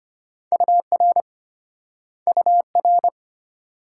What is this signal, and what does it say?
Morse code, heard as a keyed tone, about 34 wpm, 700 Hz. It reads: UR UR